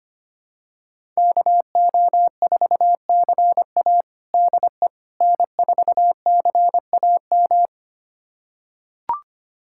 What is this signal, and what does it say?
Morse code, heard as a keyed tone, about 25 wpm, 700 Hz. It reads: KO4CA DE N4CAM
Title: USA Callsign DE Callsign - 5 Characters - 25wpm